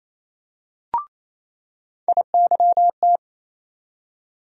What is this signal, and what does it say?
Morse code, heard as a keyed tone, about 28 wpm, 700 Hz. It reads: IYT
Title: USA Call Sign Postfixes - 3 Letters 28wpm